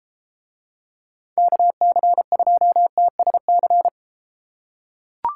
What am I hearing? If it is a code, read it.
Morse code, heard as a keyed tone, about 32 wpm, 700 Hz. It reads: KC2TSC